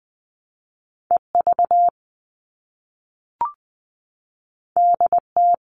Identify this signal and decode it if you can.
Morse code, heard as a keyed tone, about 20 wpm, 700 Hz. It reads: EV DT
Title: All Cut Numbers - 2-Digit 20wpm